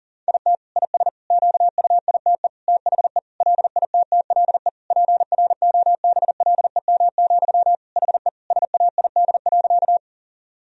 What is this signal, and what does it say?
Morse code, heard as a keyed tone, about 40 wpm, 700 Hz. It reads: IT IS QUITE THE LITTLE PROBLEM, HE SAID.
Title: Sentences from Top 1000 Words - Review Part 2 - 40wpm